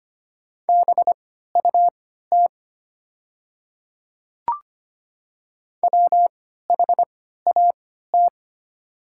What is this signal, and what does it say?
Morse code, heard as a keyed tone, about 25 wpm, spaced as for 14 wpm, 700 Hz. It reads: BUT WHAT
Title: 3x Character Spacing - Top 100 Words - 25wpm